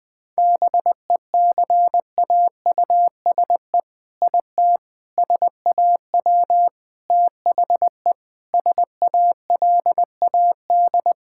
Morse code, heard as a keyed tone, about 20 wpm, 700 Hz. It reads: BECAUSE IT SAW THE SALAD
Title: All-Age Appropriate Jokes - 20wpm